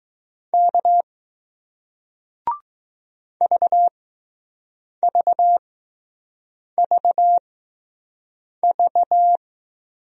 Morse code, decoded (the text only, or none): K V V V V